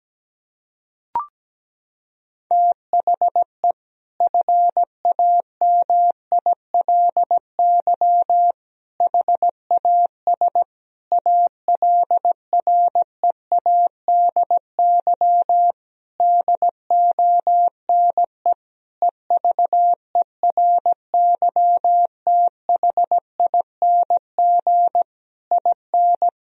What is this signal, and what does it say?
Morse code, heard as a keyed tone, about 17 wpm, 700 Hz. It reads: THE FAMILY HAS ALREADY DONE EVERYTHING IN
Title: Sentences from Top 1000 Words - Review Part 2 - 17wpm